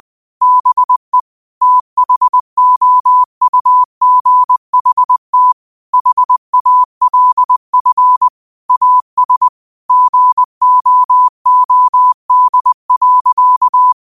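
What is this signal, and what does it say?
Morse code, heard as a keyed tone, about 20 wpm, 1000 Hz. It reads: BE THOUGHT HALF AS GOOD.
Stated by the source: Morse quote practice at 20 wpm.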